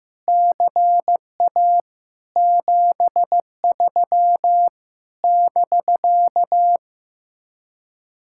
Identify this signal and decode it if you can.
Morse code, heard as a keyed tone, about 15 wpm, 700 Hz. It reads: CA 73 <BK>
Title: Speed-Racing Common QSO Phrases Encore - 15wpm